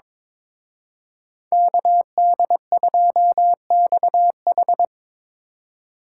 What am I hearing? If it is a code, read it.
Morse code, heard as a keyed tone, about 24 wpm, 700 Hz. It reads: KD2XH
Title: Speed Racing - USA Call Signs 22wpm